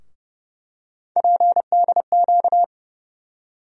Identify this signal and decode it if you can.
Morse code, heard as a keyed tone, about 30 wpm, 700 Hz. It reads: PDQ
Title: USA Call Sign Postfixes - 3 Letters 30wpm